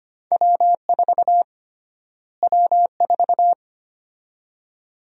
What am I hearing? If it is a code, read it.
Morse code, heard as a keyed tone, about 25 wpm, 700 Hz. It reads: W4 W4
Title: USA Call Sign Prefixes - Repeated 2x - 25wpm